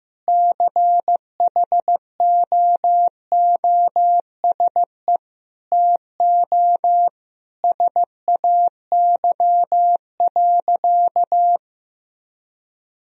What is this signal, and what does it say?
Morse code, heard as a keyed tone, about 15 wpm, 700 Hz. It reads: CHOOSE TO SAY.
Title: Sentences from Top 1000 Words - Review Part 4 - 15wpm